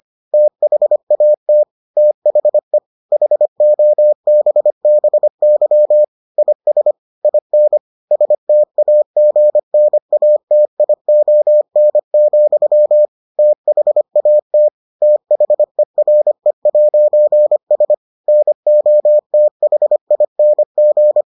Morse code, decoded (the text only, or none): THAT THE HOBBY IS IN STAGNATION, THAT THERE'S NOTHING